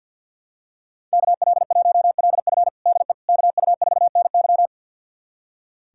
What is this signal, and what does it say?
Morse code, heard as a keyed tone, about 47 wpm, 700 Hz. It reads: KL1LF DE KU4NY
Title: USA Callsign DE Callsign - 5 Characters - 50wpm